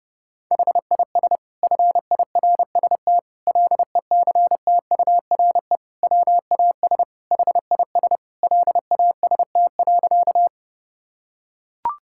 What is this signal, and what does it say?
Morse code, heard as a keyed tone, about 30 wpm, 700 Hz. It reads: HIS FIRST LECTURE WAS HIS LAST.
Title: Sentences from Top 1600 Words 30wpm